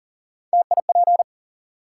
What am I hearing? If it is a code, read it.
Morse code, heard as a keyed tone, about 42 wpm, 700 Hz. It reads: TIP
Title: Three Letter Words - 40wpm